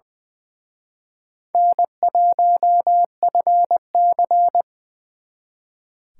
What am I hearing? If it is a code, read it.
Morse code, heard as a keyed tone, about 24 wpm, 700 Hz. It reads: N1FC
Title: Speed Racing - USA Call Signs 20wpm